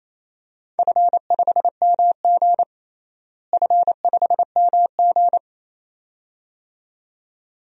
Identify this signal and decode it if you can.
Morse code, heard as a keyed tone, about 28 wpm, 700 Hz. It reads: F5MG F5MG
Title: DXCC-100 Call Signs - 4 Characters - Repeated 2x - 28wpm